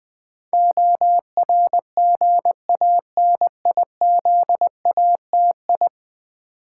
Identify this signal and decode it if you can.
Morse code, heard as a keyed tone, about 20 wpm, 700 Hz. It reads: ORGANIZATI
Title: Build A Word - 5 Letters At A Time - 12-Letter Words - 20wpm